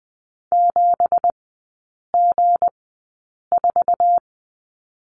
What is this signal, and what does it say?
Morse code, heard as a keyed tone, about 20 wpm, spaced as for 8 wpm, 700 Hz. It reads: 7G4
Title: 166 Lesson 27e - Number 7 - Three Characters 20wpm